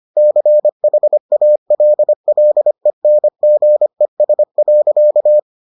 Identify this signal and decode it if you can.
Morse code, heard as a keyed tone, about 25 wpm, 600 Hz. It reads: CHALLENGES.